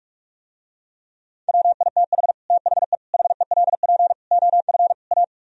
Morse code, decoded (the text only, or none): WITH THE HELP OF A